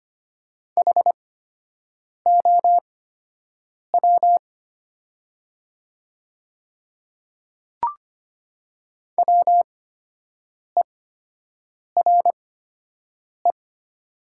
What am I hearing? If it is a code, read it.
Morse code, heard as a keyed tone, about 25 wpm, spaced as for 7 wpm, 700 Hz. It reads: HOW WERE
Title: 8x Character Spacing - Top 100 Words - 25wpm